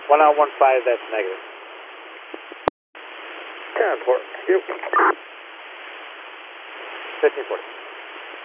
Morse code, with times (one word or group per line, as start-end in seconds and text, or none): none